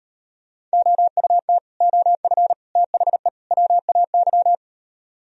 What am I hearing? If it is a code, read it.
Morse code, heard as a keyed tone, about 38 wpm, 700 Hz. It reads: OUT OF THE WAY